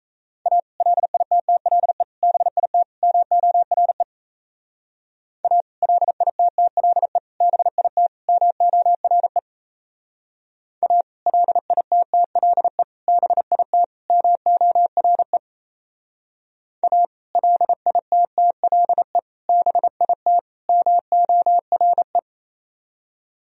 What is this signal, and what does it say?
Morse code, heard as a keyed tone, about 35 wpm, 700 Hz. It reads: A LITTLE BIT MORE A LITTLE BIT MORE A LITTLE BIT MORE A LITTLE BIT MORE
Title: Speed-Racing - Sets of 4 Words 28wpm